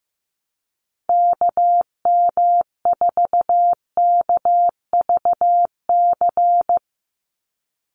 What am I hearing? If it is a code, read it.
Morse code, heard as a keyed tone, about 15 wpm, 700 Hz. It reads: KM4KVC
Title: USA Call Signs 15wpm